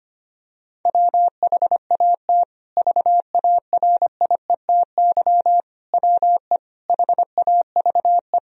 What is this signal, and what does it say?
Morse code, heard as a keyed tone, about 25 wpm, 700 Hz. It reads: WHAT VARIETY WE HAVE
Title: Sentences from Top 1000 Words - Review Part 4 - 25wpm